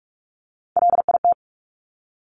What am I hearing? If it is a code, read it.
Morse code, heard as a keyed tone, about 46 wpm, 700 Hz. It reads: LIT
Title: Top Three Letter Combinations in Words 45wpm